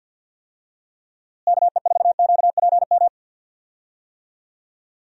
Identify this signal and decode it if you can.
Morse code, heard as a keyed tone, about 52 wpm, 700 Hz. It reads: KE4QPM
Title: USA Call Signs - 6 Characters - Repeated 3x - 50wpm